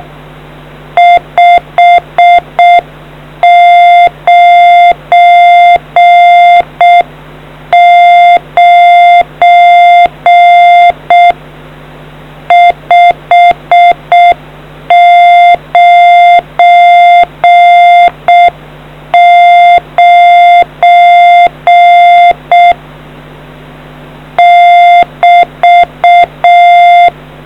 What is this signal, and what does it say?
Morse code, heard as a keyed tone, about 6 wpm, 700 Hz. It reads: 599 599 <BT>